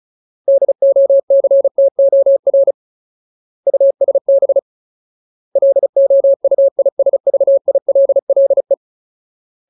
Morse code, heard as a keyed tone, about 34 wpm, 550 Hz. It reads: DOCTOR USB LOUISVILLE
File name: Ultimate Word List / 35WPM Ultimate